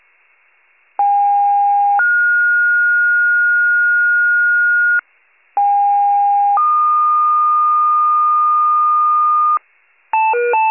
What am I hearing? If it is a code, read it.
Two-tone page, heard as a one-second tone, then a three-second tone, about 810 then 1430 Hz, about 810 then 1220 Hz, with noise. It is heard from 1.0 to 9.6 s.
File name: wcalert